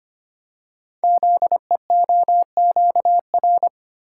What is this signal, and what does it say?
Morse code, heard as a keyed tone, about 25 wpm, 700 Hz. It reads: ZEOQR
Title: Code Groups - 5 Letters - 25wpm